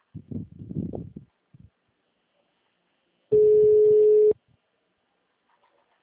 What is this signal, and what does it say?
Ringback tone (Europe, 425 Hz).
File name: out- -2017- - - .34027